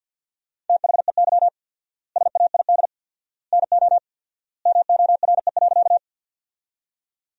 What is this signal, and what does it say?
Morse code, heard as a keyed tone, about 50 wpm, 700 Hz. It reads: THEY SAID NO MORE.
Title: 4x Word-Spacing - Sentences from Top 100 Words - 50wpm